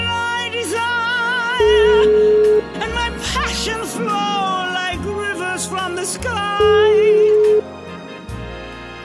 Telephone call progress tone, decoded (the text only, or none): ringback tone